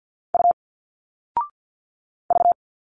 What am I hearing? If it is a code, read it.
Morse code, heard as a keyed tone, about 50 wpm, 700 Hz. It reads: U V
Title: Single Letters 50wpm